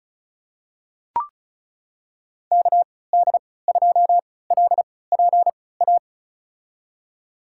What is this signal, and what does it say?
Morse code, heard as a keyed tone, about 36 wpm, spaced as for 20 wpm, 700 Hz. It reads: KD2LPA
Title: USA Call Signs - 6 Characters - 3x Character Spacing - 35wpm